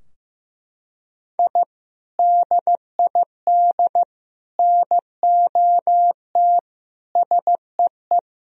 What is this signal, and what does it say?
Morse code, heard as a keyed tone, about 15 wpm, 700 Hz. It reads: I DID NOT SEE
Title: Sentences from Top 1000 Words - Review Part 4 - 15wpm